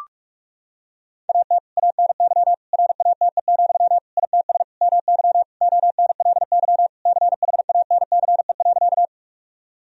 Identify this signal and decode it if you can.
Morse code, heard as a keyed tone, about 46 wpm, 700 Hz. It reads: AT ANY RATE, ITS MY ONLY CHANCE.
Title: Sentences from Top 1000 Words - Review Part 4 - 45wpm